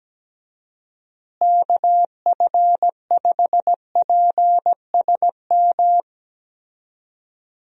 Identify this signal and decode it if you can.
Morse code, heard as a keyed tone, about 17 wpm, 700 Hz. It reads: KF5PSM
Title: USA Call Signs - 6 Characters - Repeated 2x - 17wpm